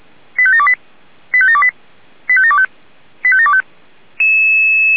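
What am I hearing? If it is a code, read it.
Five-tone selective call (ZVEI), heard as quick runs of five tones, tones of about 70 ms. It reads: 85728 85728 85725 85725
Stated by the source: ZVEI five-tone call received by SDR.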